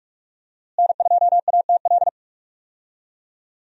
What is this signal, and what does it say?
Morse code, heard as a keyed tone, about 42 wpm, 700 Hz. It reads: N2ATL
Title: USA Call Signs - 5 Characters - Repeated 3x - 45wpm